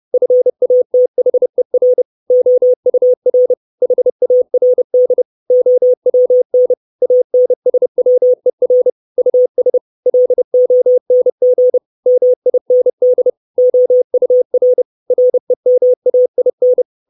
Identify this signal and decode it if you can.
Morse code, heard as a keyed tone, about 30 wpm, 500 Hz. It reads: FATHER OUR HARD OWN ANSWER US LONG MIND OUR REMAIN
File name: 632 Words 30WPM